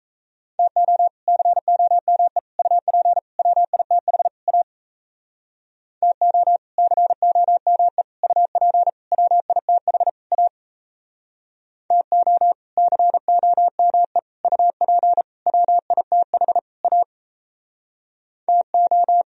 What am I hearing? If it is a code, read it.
Morse code, heard as a keyed tone, about 37 wpm, 700 Hz. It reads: TO COME UP WITH A TO COME UP WITH A TO COME UP WITH A TO